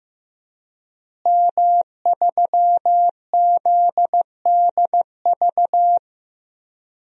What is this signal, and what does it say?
Morse code, heard as a keyed tone, about 15 wpm, 700 Hz. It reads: M3ZDV
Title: DXCC-100 Call Signs - 5 Characters - Repeated 3x - 15wpm